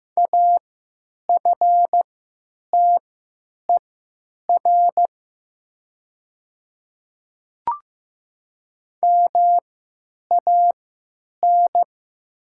Morse code, heard as a keyed tone, about 15 wpm, spaced as for 9 wpm, 700 Hz. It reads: AFTER MAN
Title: 3x Character Spacing - Top 300 Words - 15wpm